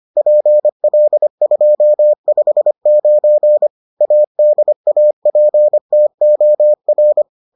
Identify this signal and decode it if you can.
Morse code, heard as a keyed tone, about 25 wpm, 600 Hz. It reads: PL259 ADAPTOR